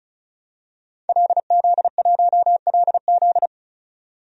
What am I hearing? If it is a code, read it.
Morse code, heard as a keyed tone, about 34 wpm, 700 Hz. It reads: LZ1LZ
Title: DXCC-100 Call Signs - 5 Characters - 35wpm